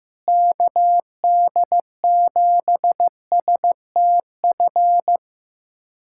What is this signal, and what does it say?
Morse code, heard as a keyed tone, about 15 wpm, 700 Hz. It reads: KD7STF
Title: USA Callsign DE Callsign 15wpm